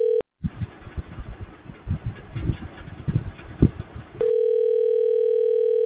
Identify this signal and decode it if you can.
Ringback tone (North America), 440 + 480 Hz.